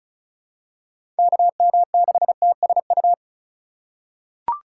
Morse code, heard as a keyed tone, about 36 wpm, 700 Hz. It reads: KM6TSU
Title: USA Call Signs - 4x Character Spacing - 35wpm